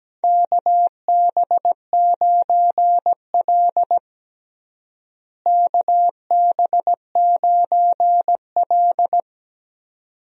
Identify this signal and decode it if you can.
Morse code, heard as a keyed tone, about 17 wpm, 700 Hz. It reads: KB9L KB9L
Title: USA Call Signs - 4 Characters - Repeated 2x - 17wpm